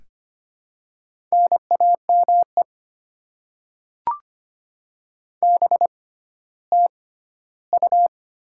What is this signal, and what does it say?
Morse code, heard as a keyed tone, about 25 wpm, spaced as for 14 wpm, 700 Hz. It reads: NAME BTU
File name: Top 20 QSO Elements - 6x Character Spacing - 25wpm